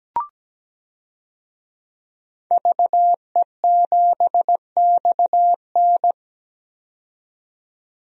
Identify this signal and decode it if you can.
Morse code, heard as a keyed tone, about 17 wpm, 700 Hz. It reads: VE7XN
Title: DXCC-100 Call Signs - 5 Characters - 17wpm